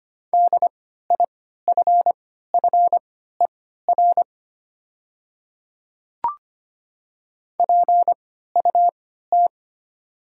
Morse code, heard as a keyed tone, about 25 wpm, spaced as for 14 wpm, 700 Hz. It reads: DIFFER PUT
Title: 3x Character Spacing - Top 200 Words - 25wpm